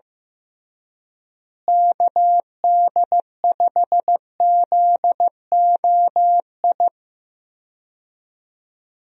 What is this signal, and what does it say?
Morse code, heard as a keyed tone, about 15 wpm, 700 Hz. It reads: KD5ZOI
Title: USA Call Signs - 6 Characters - Repeated 3x - 15wpm